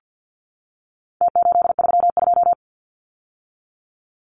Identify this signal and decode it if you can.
Morse code, heard as a keyed tone, about 49 wpm, 700 Hz. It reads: T832
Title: Numbers - Contest Style - 4-Digit 50wpm